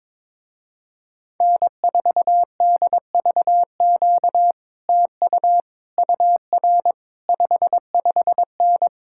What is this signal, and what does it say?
Morse code, heard as a keyed tone, about 22 wpm, 700 Hz. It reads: N4DVQ TU UR 55N
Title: Speed-Racing Common QSO Phrases - 22wpm